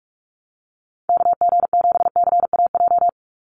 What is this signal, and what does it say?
Morse code, heard as a keyed tone, about 44 wpm, 700 Hz. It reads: KG7CAJ
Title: USA Call Signs 45wpm